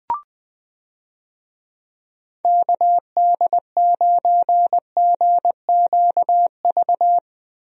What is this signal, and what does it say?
Morse code, heard as a keyed tone, about 20 wpm, 700 Hz. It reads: KD9GQV
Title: USA Call Signs - 6 Characters - Repeated 3x - 20wpm